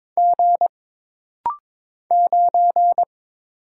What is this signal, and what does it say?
Morse code, heard as a keyed tone, about 22 wpm, 700 Hz. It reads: G 9
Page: This rapid-fire timing is geared to push your ICR (Instant Character Recognition) skills.